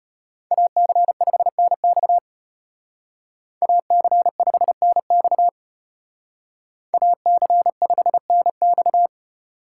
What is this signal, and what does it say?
Morse code, heard as a keyed tone, about 33 wpm, 700 Hz. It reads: AC5NX AC5NX AC5NX